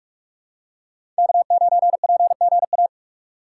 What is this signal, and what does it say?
Morse code, heard as a keyed tone, about 46 wpm, 700 Hz. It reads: K9PGA
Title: USA Call Signs - 5 Characters - 4x Character Spacing - 45wpm